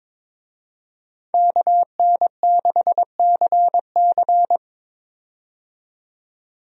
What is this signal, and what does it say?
Morse code, heard as a keyed tone, about 22 wpm, 700 Hz. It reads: KN6CC
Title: USA Call Signs - 5 Characters - 22wpm